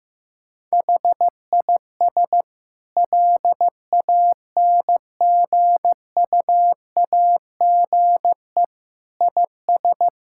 Morse code, heard as a keyed tone, about 16 wpm, 700 Hz. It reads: HIS LANGUAGE IS
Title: Sentences from Top 1600 Words 15wpm